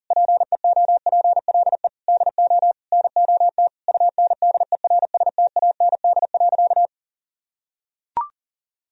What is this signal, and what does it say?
Morse code, heard as a keyed tone, about 40 wpm, 700 Hz. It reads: PEOPLE DO NOT UNDERSTAND.
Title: Sentences from Top 1000 Words - Review Part 1 - 40wpm